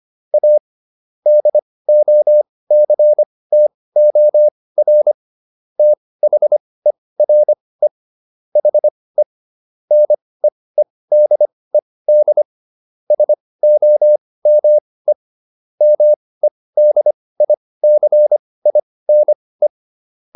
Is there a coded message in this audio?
Yes: Morse code from 0.3 to 19.7 s, heard as a keyed tone, about 25 wpm, spaced as for 18 wpm, 600 Hz. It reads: A DOCTOR THERE HE NEEDED SOME MEDICINE